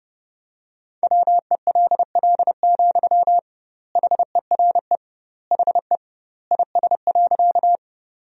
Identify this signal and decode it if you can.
Morse code, heard as a keyed tone, about 30 wpm, 700 Hz. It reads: WELL, HERE HE IS.
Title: 2x Word-Spacing - Sentences from Top 200 Words - 30wpm